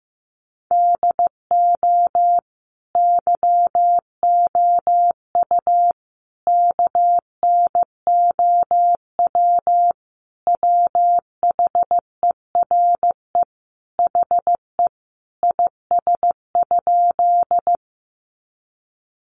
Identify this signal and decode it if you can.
Morse code, heard as a keyed tone, about 15 wpm, 700 Hz. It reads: DO YOU KNOW WHERE HE IS?
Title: Sentences from Top 300 Words 15wpm